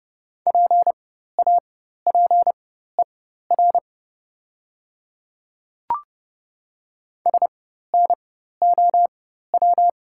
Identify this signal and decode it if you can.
Morse code, heard as a keyed tone, about 30 wpm, spaced as for 14 wpm, 690 Hz. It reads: PAPER SNOW
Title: 4x Character Spacing - Top 500 Words - 30wpm